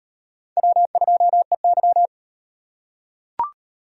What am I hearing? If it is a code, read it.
Morse code, heard as a keyed tone, about 37 wpm, 700 Hz. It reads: W2EY